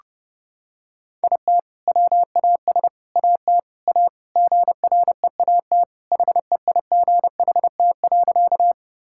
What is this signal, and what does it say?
Morse code, heard as a keyed tone, about 32 wpm, 700 Hz. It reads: IT WAS AT A GREAT HEIGHT.
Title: Sentences from Top 1600 Words 30wpm